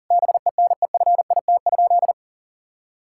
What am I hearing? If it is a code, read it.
Morse code, heard as a keyed tone, about 40 wpm, 700 Hz. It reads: BENEFIT?
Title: Sentences from Top 1000 Words - Review Part 6 - 40wpm